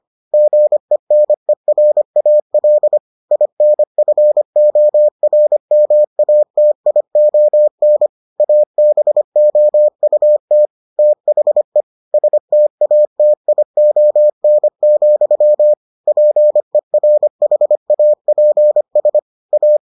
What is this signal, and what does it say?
Morse code, heard as a keyed tone, about 25 wpm, 600 Hz. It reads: GENERAL INFORMATION ABOUT THE STATION, PERHAPS A